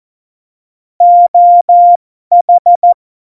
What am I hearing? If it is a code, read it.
Morse code, heard as a keyed tone, about 14 wpm, spaced as for 12 wpm, 700 Hz. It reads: OH
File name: Morsekurs / Witze